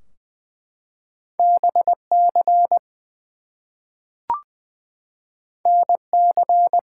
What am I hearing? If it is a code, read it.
Morse code, heard as a keyed tone, about 20 wpm, 700 Hz. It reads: BC NC
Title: US States and Canadian Provinces 20wpm